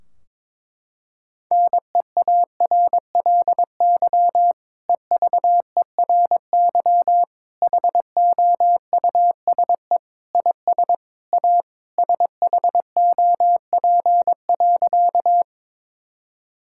Morse code, heard as a keyed tone, about 21 wpm, 700 Hz. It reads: NEARLY EVERY HOUSE IS A SHOP.
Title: Sentences from Top 1600 Words 22wpm